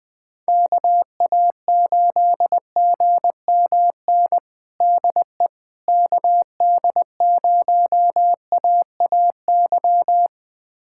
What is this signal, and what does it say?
Morse code, heard as a keyed tone, about 20 wpm, 700 Hz. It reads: KA8GMN DE KD0AAY
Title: USA Callsign DE Callsign 20wpm